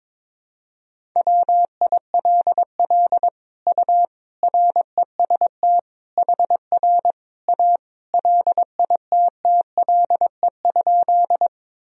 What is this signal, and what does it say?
Morse code, heard as a keyed tone, about 21 wpm, 700 Hz. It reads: WILL U REST HR A LITTLE?
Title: Sentences from Top 500 Words and Abbreviations 22wpm